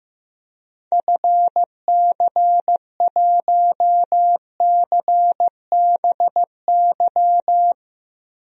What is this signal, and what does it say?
Morse code, heard as a keyed tone, about 16 wpm, 700 Hz. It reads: FC1CBY
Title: DXCC-100 Call Signs - 6 Characters - Repeated 2x - 15wpm